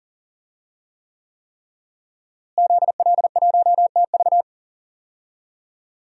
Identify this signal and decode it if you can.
Morse code, heard as a keyed tone, about 40 wpm, 700 Hz. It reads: ZL1TV
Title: DXCC-100 Call Signs - 5 Characters - Repeated 3x - 40wpm